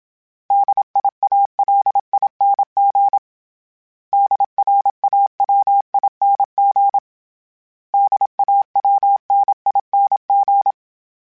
Morse code, heard as a keyed tone, about 27 wpm, 800 Hz. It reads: DIALING DRAWING DAWNING
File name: Special INGs-mp3s / Special INGs-mp3s / Sngl Ing @25